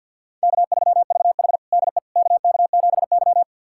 Morse code, heard as a keyed tone, about 51 wpm, 700 Hz. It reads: K3VH DE KK7Y
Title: USA Callsign DE Callsign - 4 Characters - 50wpm